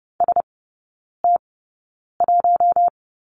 Morse code, heard as a keyed tone, about 30 wpm, spaced as for 9 wpm, 700 Hz. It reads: ST1
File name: Lesson 17e - Letter u - Three Characters 30wpm